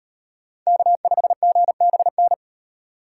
Morse code, heard as a keyed tone, about 38 wpm, 700 Hz. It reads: K5GBN